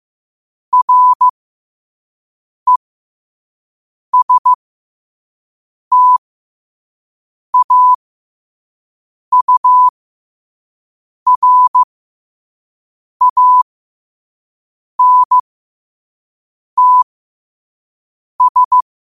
Morse code, heard as a keyed tone, about 15 wpm, spaced as for 5 wpm, 1000 Hz. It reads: RESTAURANTS
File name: qotdfiles / 5wpm